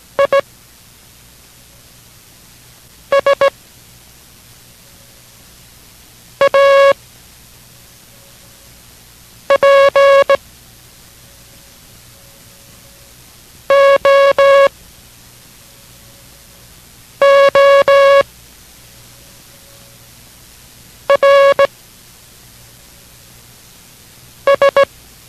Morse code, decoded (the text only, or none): IS A POORS